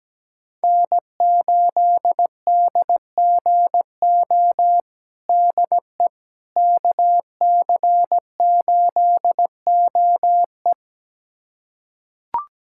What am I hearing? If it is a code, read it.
Morse code, heard as a keyed tone, about 17 wpm, 700 Hz. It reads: N8DGO DE KC8OE
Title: USA Callsign DE Callsign - 5 Characters - 17wpm